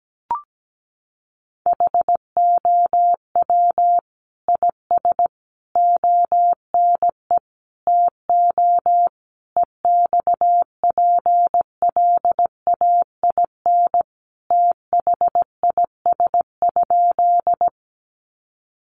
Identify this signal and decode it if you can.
Morse code, heard as a keyed tone, about 17 wpm, 700 Hz. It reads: HOW IS ONE TO EXPLAIN THIS?
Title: Sentences from Top 1100 Words 17wpm